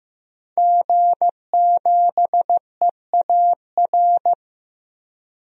Morse code, heard as a keyed tone, about 15 wpm, 700 Hz. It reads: G7EAR
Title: DXCC-100 Call Signs - 5 Characters - 15wpm